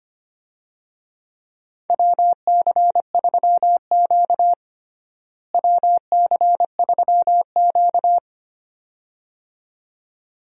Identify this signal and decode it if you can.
Morse code, heard as a keyed tone, about 25 wpm, 700 Hz. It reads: WC3Q WC3Q
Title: USA Call Signs - 4 Characters - Repeated 2x - 25wpm